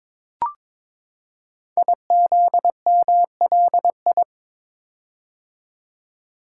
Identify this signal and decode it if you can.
Morse code, heard as a keyed tone, about 22 wpm, 700 Hz. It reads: IZMLI
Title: Code Groups - 5 Letters - 22wpm